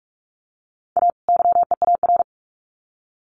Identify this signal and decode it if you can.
Morse code, heard as a keyed tone, about 44 wpm, 700 Hz. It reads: A YEAR
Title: Sets of 2 Words 45wpm